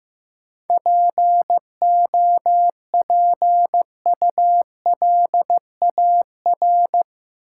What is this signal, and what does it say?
Morse code, heard as a keyed tone, about 15 wpm, 700 Hz. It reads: POPULAR
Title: Sentences from Top 1000 Words - Review Part 3 - New Word Drill - 15wpm